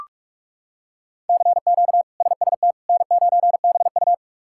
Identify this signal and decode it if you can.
Morse code, heard as a keyed tone, about 46 wpm, 700 Hz. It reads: CQ SST N9BU
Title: K1USN SST QSO Practice - 45wpm